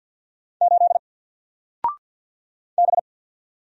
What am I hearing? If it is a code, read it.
Morse code, heard as a keyed tone, about 48 wpm, 700 Hz. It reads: 8 B
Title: Single Letter-Number Alt 50wpm